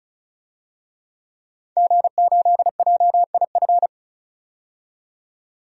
Morse code, heard as a keyed tone, about 36 wpm, 700 Hz. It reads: G8JIF
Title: DXCC-100 Call Signs - 5 Characters - 35wpm